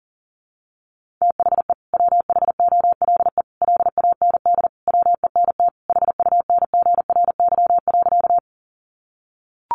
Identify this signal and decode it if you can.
Morse code, heard as a keyed tone, about 40 wpm, 700 Hz. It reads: THE WHOLE LAND WENT HUNGRY.
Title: Sentences from Top 1400 Words 40wpm